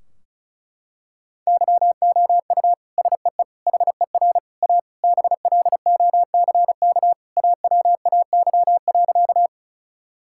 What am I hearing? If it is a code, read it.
Morse code, heard as a keyed tone, about 36 wpm, 700 Hz. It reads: YOU SEE HER A BLOCK AWAY.
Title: Sentences from Top 1600 Words 35wpm